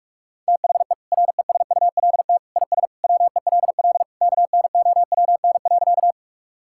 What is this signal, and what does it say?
Morse code, heard as a keyed tone, about 46 wpm, 700 Hz. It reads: THE RESULT IS WELL KNOWN.
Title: Sentences from Top 1000 Words - Review Part 5 - 45wpm